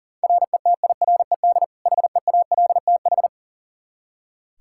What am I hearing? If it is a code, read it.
Morse code, heard as a keyed tone, about 40 wpm, 700 Hz. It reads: RETIRED HEALTH
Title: Practice QSO - Exchange 3 - More Info - 40wpm